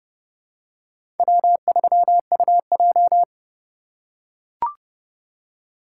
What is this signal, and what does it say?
Morse code, heard as a keyed tone, about 30 wpm, 700 Hz. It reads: W3UJ
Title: USA Call Signs - 4 Characters - 2x Character Spacing - 30wpm